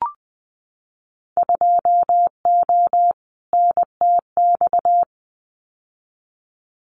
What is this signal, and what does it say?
Morse code, heard as a keyed tone, about 20 wpm, 700 Hz. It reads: 2O NTX
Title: Winter Field Day Exchanges 20wpm